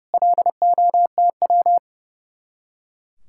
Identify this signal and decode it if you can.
Morse code, heard as a keyed tone, about 39 wpm, 700 Hz. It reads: LOTW